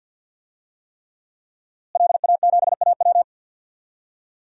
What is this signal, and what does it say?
Morse code, heard as a keyed tone, about 52 wpm, 700 Hz. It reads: LA7AW
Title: DXCC-100 Call Signs - 5 Characters - Repeated 3x - 50wpm